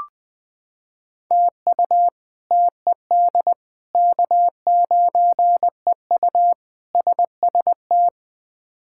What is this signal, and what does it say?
Morse code, heard as a keyed tone, about 20 wpm, 700 Hz. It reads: TU TED K9EU SST
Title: K1USN SST QSO Practice - 20wpm